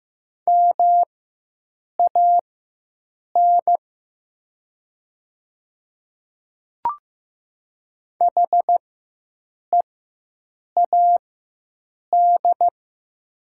Morse code, heard as a keyed tone, about 15 wpm, spaced as for 7 wpm, 700 Hz. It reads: MAN HEAD
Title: 4x Character Spacing - Top 300 Words - 15wpm